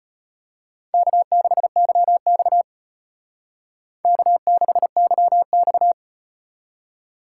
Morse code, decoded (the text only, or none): K6YX K6YX